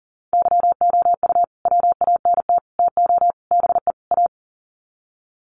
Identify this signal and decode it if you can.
Morse code, heard as a keyed tone, about 40 wpm, 700 Hz. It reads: YOU WANT TO BE A
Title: Sets of 5 Words Encore2 40wpm